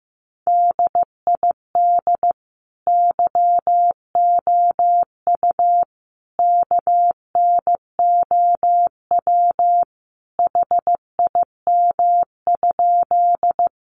Morse code, heard as a keyed tone, about 15 wpm, 700 Hz. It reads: DID YOU KNOW HIM?
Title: Sentences from Top 100 Words 15wpm